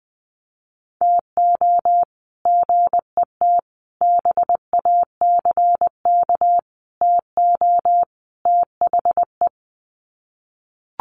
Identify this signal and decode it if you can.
Morse code, heard as a keyed tone, about 20 wpm, 700 Hz. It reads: TO GET BACK TO THE
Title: Sets of 5 Words Encore 20wpm